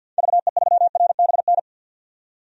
Morse code, heard as a keyed tone, about 47 wpm, 700 Hz. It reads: VE3RBN
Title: DXCC-100 Call Signs - 6 Characters - 50wpm